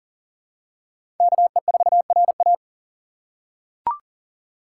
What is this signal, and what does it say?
Morse code, heard as a keyed tone, about 40 wpm, 700 Hz. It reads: KE4RA
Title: USA Call Signs - 4x Character Spacing - 40wpm